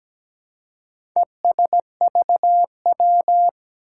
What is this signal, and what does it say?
Morse code, heard as a keyed tone, about 17 wpm, 700 Hz. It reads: ESVW
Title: Four Challenging Letter-Number Combinations 17wpm